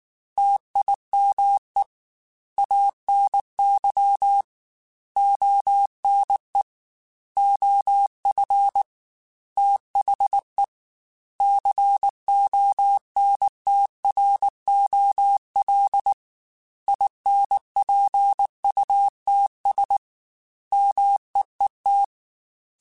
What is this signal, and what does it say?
Morse code, heard as a keyed tone, about 19 wpm, 800 Hz. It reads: TIME ANY ONE OF THE CONTROL INPUTS MEET
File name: Technical Subjects / Technical Subjects-mp3s / OR Gate @18